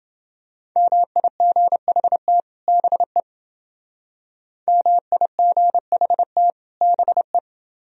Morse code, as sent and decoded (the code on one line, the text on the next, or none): -- .. --. .... - / -... / -- .. --. .... - / -... .
MIGHT BE MIGHT BE